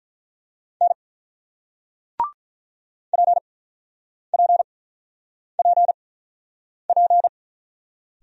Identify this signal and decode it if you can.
Morse code, heard as a keyed tone, about 47 wpm, spaced as for 9 wpm, 700 Hz. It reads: N PPPP